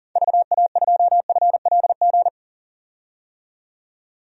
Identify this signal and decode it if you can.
Morse code, heard as a keyed tone, about 40 wpm, 700 Hz. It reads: VA2FLG
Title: DXCC-100 Call Signs - 6 Characters - 40wpm